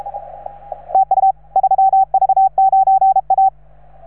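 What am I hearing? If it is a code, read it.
Morse code, heard as a keyed tone, about 36 wpm, 770 Hz. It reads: TU 3V9A
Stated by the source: Morse from station 3V9A.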